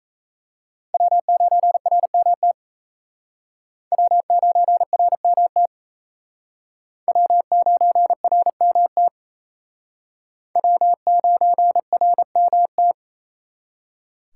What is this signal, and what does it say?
Morse code, heard as a keyed tone, about 35 wpm, 700 Hz. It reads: W9RMT W9RMT W9RMT W9RMT